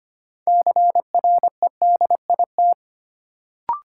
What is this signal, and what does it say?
Morse code, heard as a keyed tone, about 25 wpm, 700 Hz. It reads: CREDIT
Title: Sentences from Top 1000 Words - New Word Drill - 25wpm